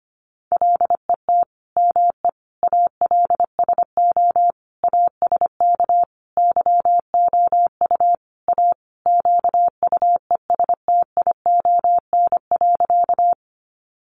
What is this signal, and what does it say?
Morse code, heard as a keyed tone, about 25 wpm, 700 Hz. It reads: LET ME ALSO ASK YOU A QUESTION.
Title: Sentences from Top 400 Words 25wpm